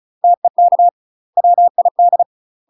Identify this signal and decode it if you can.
Morse code, heard as a keyed tone, about 34 wpm, 700 Hz. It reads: TEK WID
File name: UK MASTER QSOs TRAINING / 35WPM / UK MASTER QSOs 0003